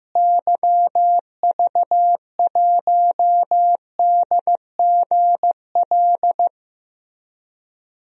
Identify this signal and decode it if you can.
Morse code, heard as a keyed tone, about 15 wpm, 700 Hz. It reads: YV1DGL
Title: DXCC-100 Call Signs - 6 Characters - Repeated 3x - 15wpm